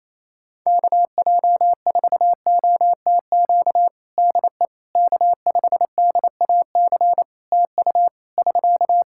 Morse code, heard as a keyed tone, about 27 wpm, 700 Hz. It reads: KJ4OTQ DE K5DAC TU <SK>
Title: Practice QSO - Exchange 4 - Ending - 28wpm